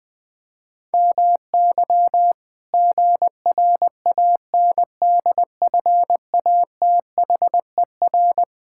Morse code, heard as a keyed tone, about 20 wpm, 700 Hz. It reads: MY GRANDFATHER
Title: Pangram Challenge v2 - 20wpm